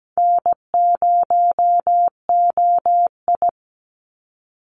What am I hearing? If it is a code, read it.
Morse code, heard as a keyed tone, about 17 wpm, 700 Hz. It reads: N0OI
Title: NA SOTA Activator Callsigns 17wpm